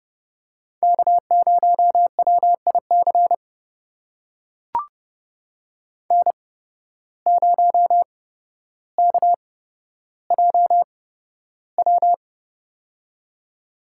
Morse code, heard as a keyed tone, about 30 wpm, spaced as for 13 wpm, 700 Hz. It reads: K0WIC N0KJW
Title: USA Call Signs - 5 Characters - 8x Character Spacing - 30wpm